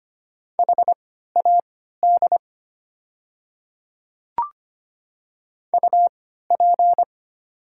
Morse code, heard as a keyed tone, about 25 wpm, spaced as for 14 wpm, 700 Hz. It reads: HAD UP
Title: 3x Character Spacing - Top 300 Words - 25wpm